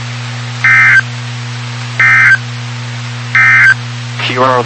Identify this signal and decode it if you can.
Emergency Alert System (SAME) tones, heard as harsh buzzing bursts, with speech. It is heard from 0.5 to 3.8 s.